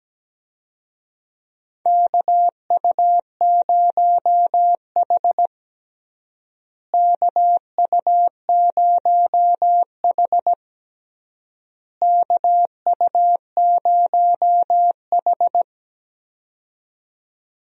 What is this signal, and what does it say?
Morse code, heard as a keyed tone, about 17 wpm, 700 Hz. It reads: KU0H KU0H KU0H
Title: USA Call Signs - 4 Characters - Repeated 3x - 17wpm